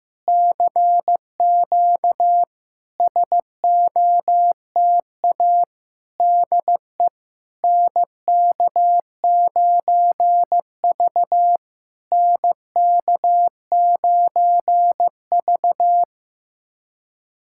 Morse code, heard as a keyed tone, about 16 wpm, 700 Hz. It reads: CQ SOTA DE NK9V NK9V
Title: SOTA QSO Practice 15wpm